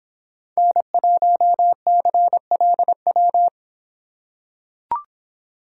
Morse code, heard as a keyed tone, about 25 wpm, 700 Hz. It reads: N1CLW